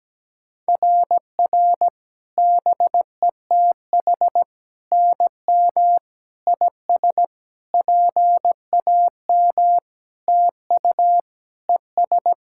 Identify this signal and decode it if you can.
Morse code, heard as a keyed tone, about 17 wpm, 700 Hz. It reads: RR BETH NM IS PAM TU ES
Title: SOTA QSO Practice 17wpm